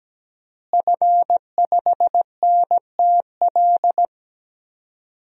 Morse code, heard as a keyed tone, about 17 wpm, 700 Hz. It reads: F5NTL
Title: DXCC-100 Call Signs - 5 Characters - Repeated 2x - 17wpm